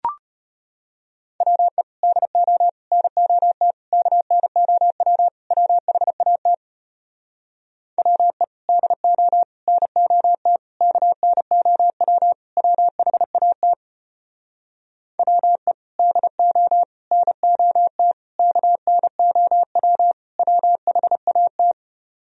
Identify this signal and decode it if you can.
Morse code, heard as a keyed tone, about 34 wpm, 700 Hz. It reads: WE DO NOT KNOW WHAT WE DO NOT KNOW WHAT WE DO NOT KNOW WHAT